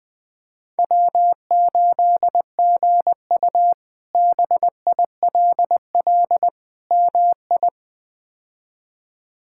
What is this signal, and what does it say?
Morse code, heard as a keyed tone, about 21 wpm, 700 Hz. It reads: W8GU BILL MI
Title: K1USN SST QSO Practice - 20wpm